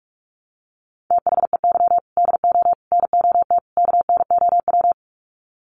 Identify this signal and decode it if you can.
Morse code, heard as a keyed tone, about 44 wpm, 700 Hz. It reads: THEY DO NOT KNOW
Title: Sets of 4 Words 45wpm